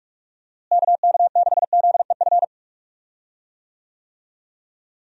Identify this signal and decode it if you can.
Morse code, heard as a keyed tone, about 43 wpm, 700 Hz. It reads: KK6ZEF
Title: USA Call Signs - 6 Characters - Repeated 3x - 45wpm